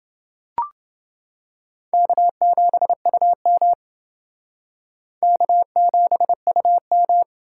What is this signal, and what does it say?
Morse code, heard as a keyed tone, about 28 wpm, 700 Hz. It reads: K7UM K7UM